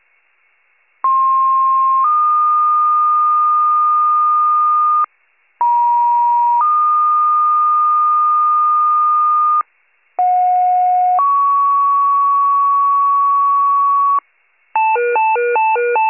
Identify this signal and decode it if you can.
Two-tone page, heard as a one-second tone, then a three-second tone, about 1060 then 1220 Hz, about 950 then 1220 Hz, about 730 then 1120 Hz. It is heard from 1.0 to 14.2 s.